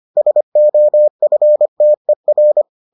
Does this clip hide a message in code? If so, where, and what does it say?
Yes: Morse code from 0.2 to 2.6 s, heard as a keyed tone, about 25 wpm, 600 Hz. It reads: SOFTER